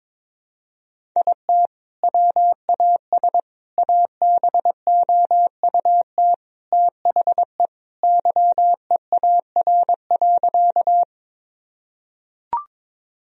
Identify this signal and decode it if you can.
Morse code, heard as a keyed tone, about 21 wpm, 700 Hz. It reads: IT WAS ABOUT THE YEAR.
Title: Sentences from Top 1000 Words - Review Part 1 - 22wpm